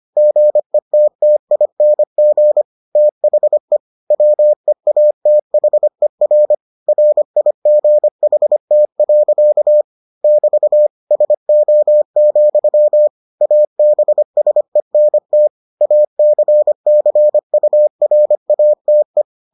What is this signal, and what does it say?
Morse code, heard as a keyed tone, about 25 wpm, 600 Hz. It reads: GETTING THE WEATHER RIGHT. <BT> SO, ABSENT ACCURATE